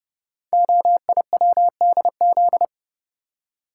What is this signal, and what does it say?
Morse code, heard as a keyed tone, about 30 wpm, 700 Hz. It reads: OIWDZ
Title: Code Groups - 5 Letters - 30wpm